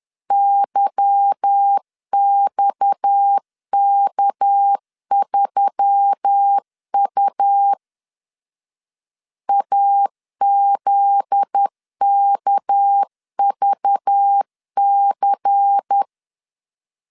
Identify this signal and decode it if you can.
Morse code, heard as a keyed tone, about 10 wpm, 800 Hz. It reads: YXK3U AZKVC
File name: CW / DET / 1 9 1500C S C 10wpm 4557